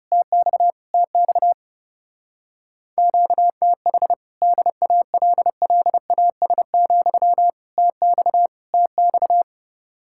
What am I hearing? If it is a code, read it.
Morse code, heard as a keyed tone, about 28 wpm, 700 Hz. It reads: TX TX QTH DALLAS, TX TX
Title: Speed-Racing Common QSO Phrases - 30wpm